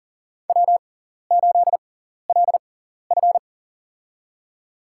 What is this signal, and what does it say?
Morse code, heard as a keyed tone, about 40 wpm, spaced as for 14 wpm, 700 Hz. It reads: W8LF
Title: USA Call Signs - 4 Characters - 6x Character Spacing - 40wpm